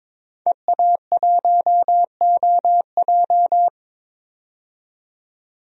Morse code, heard as a keyed tone, about 22 wpm, 700 Hz. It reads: EA1OJ
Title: DXCC-100 Call Signs - 5 Characters - 22wpm